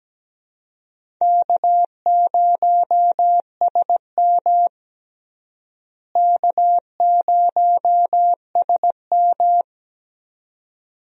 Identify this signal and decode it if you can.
Morse code, heard as a keyed tone, about 17 wpm, 700 Hz. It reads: K0SM K0SM
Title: USA Call Signs - 4 Characters - Repeated 2x - 17wpm